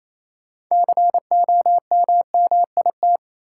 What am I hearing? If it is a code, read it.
Morse code, heard as a keyed tone, about 28 wpm, 700 Hz. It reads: COMMIT
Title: Six Letter Words - 28wpm